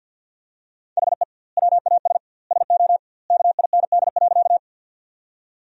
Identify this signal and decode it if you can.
Morse code, heard as a keyed tone, about 50 wpm, 700 Hz. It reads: HE WAS SO KIND.
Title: 2x Word-Spacing - Sentences from Top 200 Words - 50wpm